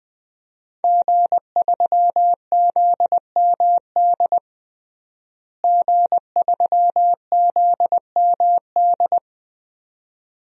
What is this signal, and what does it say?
Morse code, heard as a keyed tone, about 20 wpm, 700 Hz. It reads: G3ZMD G3ZMD
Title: DXCC-100 Call Signs - 5 Characters - Repeated 2x - 20wpm